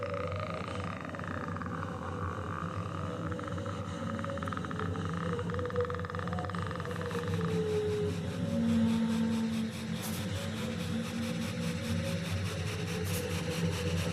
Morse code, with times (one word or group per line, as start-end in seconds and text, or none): none